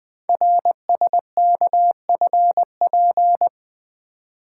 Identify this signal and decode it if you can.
Morse code, heard as a keyed tone, about 20 wpm, 700 Hz. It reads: RSKFP
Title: Code Groups - 5 Letters - 20wpm